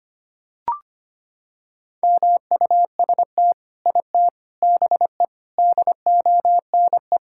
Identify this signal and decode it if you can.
Morse code, heard as a keyed tone, about 25 wpm, 700 Hz. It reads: MUST IT BE DONE
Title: Sentences from Top 1000 Words - Review Part 5 - 25wpm